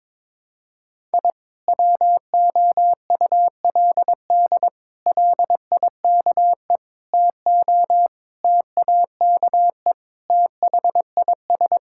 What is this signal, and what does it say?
Morse code, heard as a keyed tone, about 22 wpm, 700 Hz. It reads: I WOULD LIKE TO TAKE THIS
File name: practice / 22wpm / Sentences with 8-Word Phrases - 22wpm